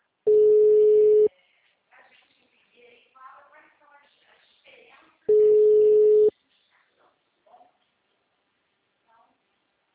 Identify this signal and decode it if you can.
Ringback tone (Europe, 425 Hz), with speech.
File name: out- -2014- - - .22